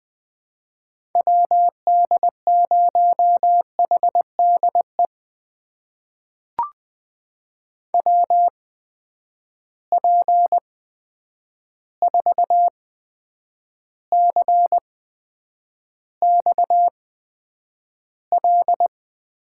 Morse code, heard as a keyed tone, about 21 wpm, spaced as for 9 wpm, 700 Hz. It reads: WD0HDE WP4CXL
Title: USA Call Signs - 6 Characters - 8x Character Spacing - 20wpm